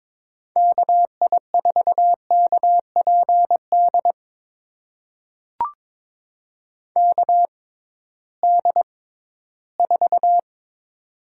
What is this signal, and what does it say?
Morse code, heard as a keyed tone, about 22 wpm, spaced as for 14 wpm, 700 Hz. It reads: KI4KPD KD4
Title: USA Call Signs - 6 Characters - 6x Character Spacing - 22wpm